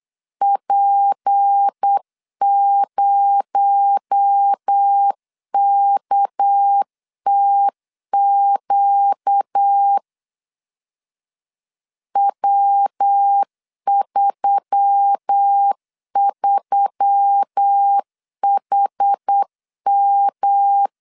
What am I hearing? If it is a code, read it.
Morse code, heard as a keyed tone, about 8 wpm, 800 Hz. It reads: P0KTQ W33HM
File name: 8wpm